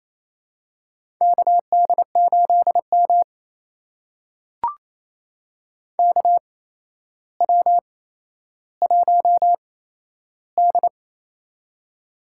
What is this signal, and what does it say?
Morse code, heard as a keyed tone, about 28 wpm, spaced as for 12 wpm, 700 Hz. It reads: KD8M KW1D
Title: USA Call Signs - 4 Characters - 8x Character Spacing - 28wpm